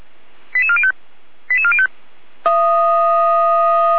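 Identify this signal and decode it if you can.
Five-tone selective call (ZVEI), heard as quick runs of five tones, tones of about 70 ms. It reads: 88405 88405
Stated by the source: ZVEI five-tone selective call 88104 received via SDR, archived on the Feuerwehr Orlamünde (f85360) audio server.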